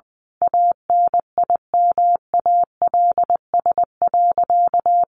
Morse code, heard as a keyed tone, about 20 wpm, 700 Hz. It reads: ANIMALS.
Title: A Sound Of Thunder 20wpm